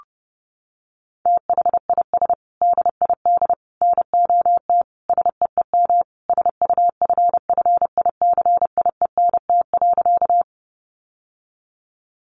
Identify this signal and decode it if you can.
Morse code, heard as a keyed tone, about 30 wpm, 700 Hz. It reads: THIS DID NOT SEEM SUFFICIENT.
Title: Sentences from Top 1400 Words 30wpm